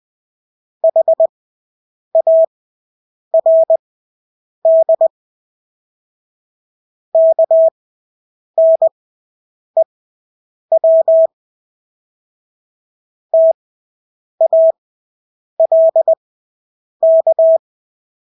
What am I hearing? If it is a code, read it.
Morse code, heard as a keyed tone, about 20 wpm, spaced as for 8 wpm, 650 Hz. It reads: HARD KNEW TALK
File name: Engelske ord 20/8 wpm